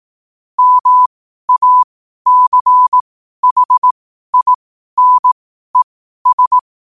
Morse code, heard as a keyed tone, about 18 wpm, spaced as for 12 wpm, 1000 Hz. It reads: MACHINES
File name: qotdfiles / 12wpm